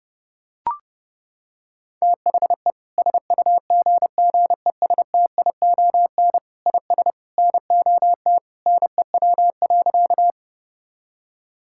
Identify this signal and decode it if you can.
Morse code, heard as a keyed tone, about 32 wpm, 700 Hz. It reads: THE SUGGESTION IS NOT NEW.
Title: Sentences from Top 1600 Words 30wpm